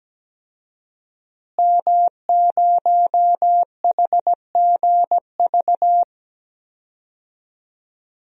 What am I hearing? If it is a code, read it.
Morse code, heard as a keyed tone, about 17 wpm, 700 Hz. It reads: M0HGV
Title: DXCC-100 Call Signs - 5 Characters - Repeated 3x - 17wpm